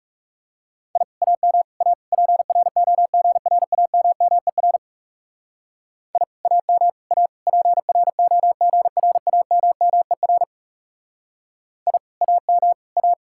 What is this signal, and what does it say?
Morse code, heard as a keyed tone, about 37 wpm, 700 Hz. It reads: I AM A PROGRAMMER I AM A PROGRAMMER I AM A
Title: Speed-Racing Common QSO Phrases Encore - 30wpm